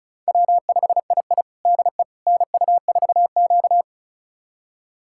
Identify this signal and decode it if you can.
Morse code, heard as a keyed tone, about 34 wpm, 700 Hz. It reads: W5II DE NU4Q
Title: USA Callsign DE Callsign - 4 Characters - 35wpm